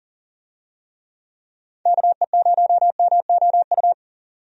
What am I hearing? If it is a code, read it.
Morse code, heard as a keyed tone, about 38 wpm, 700 Hz. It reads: KE0MOU
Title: USA Call Signs - 6 Characters - 40wpm